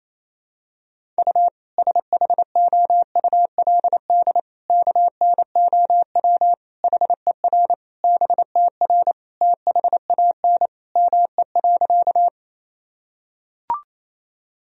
Morse code, heard as a keyed tone, about 27 wpm, 700 Hz. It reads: U SHOULD KNOW HER BTR THAN ME.
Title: Sentences from Top 400 Words and Abbreviations 28wpm